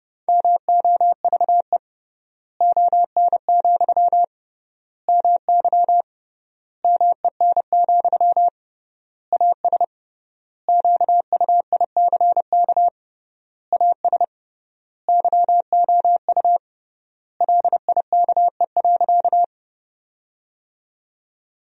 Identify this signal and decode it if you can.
Morse code, heard as a keyed tone, about 30 wpm, 700 Hz. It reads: MOVE ON, MY MEN, AS QUICK AS YOU LIKE.
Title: 3x Word-Spacing - Sentences from Top 500 Words - 30wpm